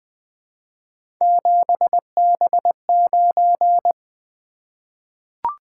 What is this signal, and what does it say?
Morse code, heard as a keyed tone, about 20 wpm, 700 Hz. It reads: 7B9
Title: Three Letter-Number Combinations - 20wpm